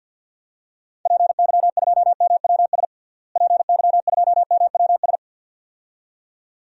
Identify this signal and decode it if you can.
Morse code, heard as a keyed tone, about 53 wpm, 700 Hz. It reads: PY2MWS PY2MWS
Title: DXCC-100 Call Signs - 6 Characters - Repeated 2x - 50wpm